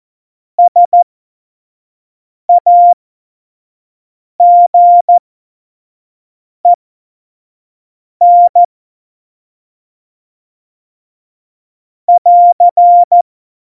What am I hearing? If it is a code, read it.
Morse code, heard as a keyed tone, about 14 wpm, spaced as for 5 wpm, 700 Hz. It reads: SAGEN <AR>
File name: Morsekurs / Witze / Witz 03. Oktober 17, Tempo 25 F70